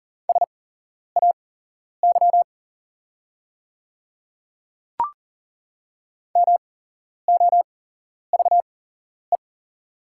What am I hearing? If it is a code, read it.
Morse code, heard as a keyed tone, about 40 wpm, spaced as for 11 wpm, 700 Hz. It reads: SAY MOVE
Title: 8x Character Spacing - Top 200 Words - 40wpm